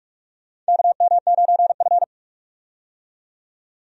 Morse code, heard as a keyed tone, about 44 wpm, 700 Hz. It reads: KM9F
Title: USA Call Signs - 4 Characters - 45wpm